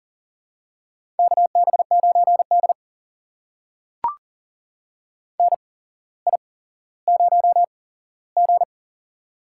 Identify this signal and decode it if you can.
Morse code, heard as a keyed tone, about 40 wpm, spaced as for 17 wpm, 700 Hz. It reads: KB9D NI0G
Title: USA Call Signs - 4 Characters - 8x Character Spacing - 40wpm